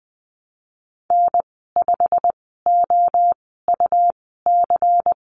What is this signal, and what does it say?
Morse code, heard as a keyed tone, about 20 wpm, spaced as for 15 wpm, 700 Hz. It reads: N5OUC
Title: 154 Lesson 25g - Letter p - Call Signs 20wpm